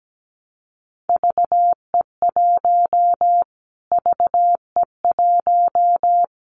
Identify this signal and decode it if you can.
Morse code, heard as a keyed tone, about 17 wpm, 700 Hz. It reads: VE1 VE1
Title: Callsign Prefixes for Top 25 DXCC - Repeated 3x - 17wpm